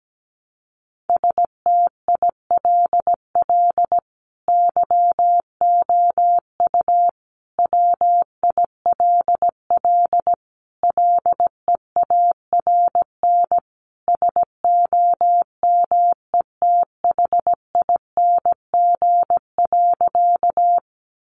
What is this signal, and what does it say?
Morse code, heard as a keyed tone, about 17 wpm, 700 Hz. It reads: STILL YOU WILL LEARN SOMETHING.
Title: Sentences from Top 600 Words 17wpm